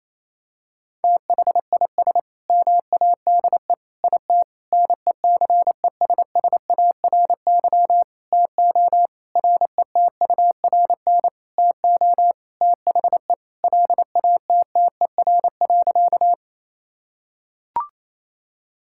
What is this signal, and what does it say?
Morse code, heard as a keyed tone, about 27 wpm, 700 Hz. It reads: THIS MADE IT NECESSARY TO RETURN TO THE LATTER.
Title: Sentences from Top 1600 Words 28wpm